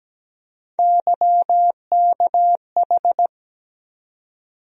Morse code, heard as a keyed tone, about 17 wpm, 700 Hz. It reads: YKH
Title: USA Call Sign Postfixes - 3 Letters 17wpm